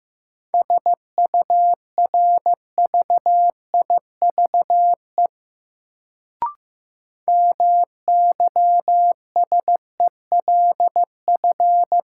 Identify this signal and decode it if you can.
Morse code, heard as a keyed tone, about 15 wpm, 700 Hz. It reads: SURVIVE MYSELF
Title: Sentences from Top 1300 Words - New Word Drill - 15wpm